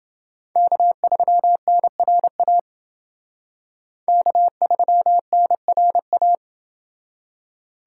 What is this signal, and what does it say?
Morse code, heard as a keyed tone, about 29 wpm, 700 Hz. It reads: K3NRA K3NRA